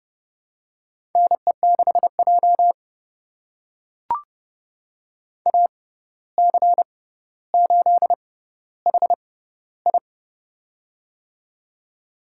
Morse code, heard as a keyed tone, about 32 wpm, spaced as for 15 wpm, 700 Hz. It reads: NE6J AC8HI
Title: USA Call Signs - 6x Character Spacing - 30wpm